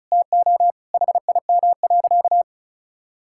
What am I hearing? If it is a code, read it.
Morse code, heard as a keyed tone, about 36 wpm, 700 Hz. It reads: TO HIM.
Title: Sentences from Top 1600 Words 35wpm